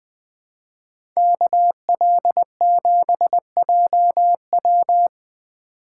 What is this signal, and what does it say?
Morse code, heard as a keyed tone, about 23 wpm, 700 Hz. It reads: KL7JW
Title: Speed Racing - USA 2x2 Call Signs 20wpm